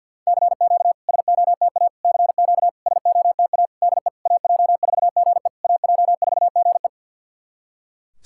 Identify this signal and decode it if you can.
Morse code, heard as a keyed tone, about 51 wpm, 690 Hz. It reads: CQ SOTA CQ SOTA DE AJ4GE AJ4GE
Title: SOTA QSO Practice 50wpm